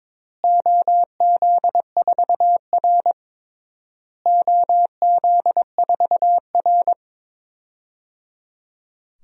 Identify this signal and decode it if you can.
Morse code, heard as a keyed tone, about 22 wpm, 700 Hz. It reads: OZ4R OZ4R
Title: DXCC-100 Call Signs - 4 Characters - Repeated 3x - 22wpm